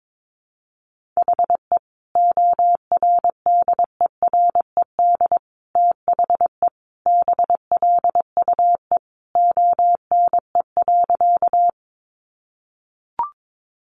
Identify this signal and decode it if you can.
Morse code, heard as a keyed tone, about 22 wpm, 700 Hz. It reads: HE ORDERED THE BLUE ONE.
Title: Additional Easy Phrases 22wpm